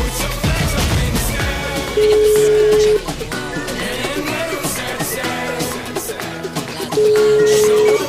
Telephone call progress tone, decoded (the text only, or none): ringback tone